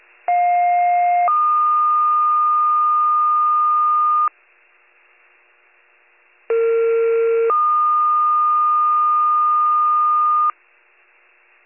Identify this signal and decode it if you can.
Two-tone page, heard as a one-second tone, then a three-second tone, about 710 then 1220 Hz, about 460 then 1220 Hz. It is heard from 0.3 to 10.5 s.